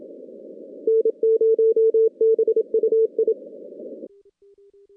Morse code, heard as a keyed tone, about 27 wpm, 440 Hz. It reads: N0BUI
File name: contests / CW-SS-k5tr- -2100